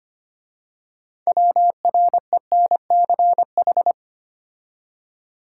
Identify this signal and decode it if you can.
Morse code, heard as a keyed tone, about 30 wpm, 700 Hz. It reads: WRENCH
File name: Easy Words - Speed-Racing with Hints - 25wpm